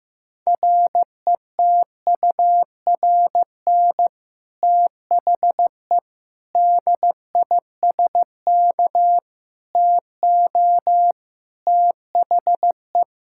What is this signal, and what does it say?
Morse code, heard as a keyed tone, about 15 wpm, 700 Hz. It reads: RETURN THE DISK TO THE
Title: Sentences from Top 1000 Words - Review Part 6 - 15wpm